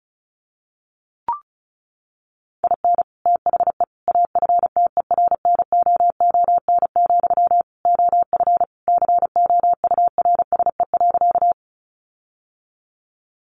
Morse code, heard as a keyed tone, about 34 wpm, 700 Hz. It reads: IN THE AFTERNOON, OF COURSE.
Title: Sentences from Top 1500 Words 35wpm